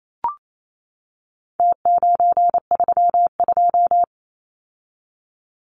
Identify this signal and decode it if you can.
Morse code, heard as a keyed tone, about 28 wpm, 700 Hz. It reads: T932
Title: Numbers - Contest Style - 4-Digit 28wpm